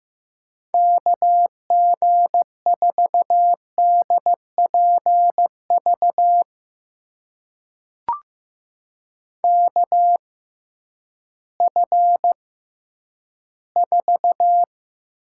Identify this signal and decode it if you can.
Morse code, heard as a keyed tone, about 15 wpm, spaced as for 10 wpm, 700 Hz. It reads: KG4DPV KF4
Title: USA Call Signs - 6 Characters - 6x Character Spacing - 15wpm